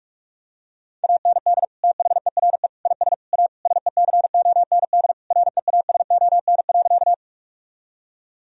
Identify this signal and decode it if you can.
Morse code, heard as a keyed tone, about 43 wpm, 700 Hz. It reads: AND THERE IS A SECOND REASON.
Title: Sentences from Top 1000 Words - Review Part 1 - 45wpm